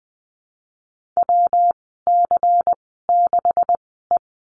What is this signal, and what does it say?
Morse code, heard as a keyed tone, about 20 wpm, spaced as for 14 wpm, 700 Hz. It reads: WC6E
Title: 140 Lesson 23g - Letter f - Call Signs 20wpm